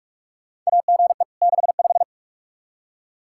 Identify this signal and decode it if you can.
Morse code, heard as a keyed tone, about 48 wpm, 700 Hz. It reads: AGE 65
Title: K1USN SST QSO Practice 45wpm